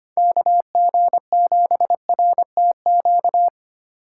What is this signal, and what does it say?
Morse code, heard as a keyed tone, about 25 wpm, 700 Hz. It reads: KG7RTQ
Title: USA Call Signs - 6 Characters - 2x Character Spacing - 25wpm